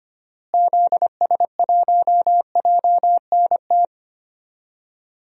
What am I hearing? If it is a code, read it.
Morse code, heard as a keyed tone, about 25 wpm, 700 Hz. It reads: ZS1JNT
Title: DXCC-100 Call Signs - 6 Characters - Repeated 3x - 25wpm